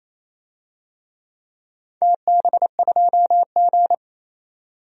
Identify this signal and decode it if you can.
Morse code, heard as a keyed tone, about 28 wpm, 700 Hz. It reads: TB2G
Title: DXCC-100 Call Signs - 4 Characters - 28wpm